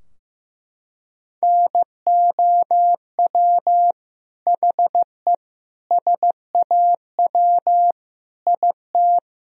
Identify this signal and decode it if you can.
Morse code, heard as a keyed tone, about 15 wpm, 700 Hz. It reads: NOW HE SAW IT
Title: Sentences from Top 1000 Words - Review Part 4 - 15wpm